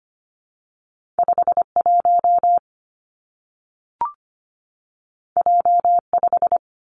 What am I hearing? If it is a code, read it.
Morse code, heard as a keyed tone, about 25 wpm, 700 Hz. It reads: 5 1 J 5
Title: Two Challenging Letter-Number Combinations 25wpm